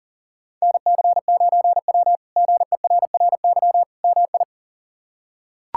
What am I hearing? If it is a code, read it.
Morse code, heard as a keyed tone, about 40 wpm, 700 Hz. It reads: NC9W GERRY MI
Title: K1USN SST QSO Practice - 40wpm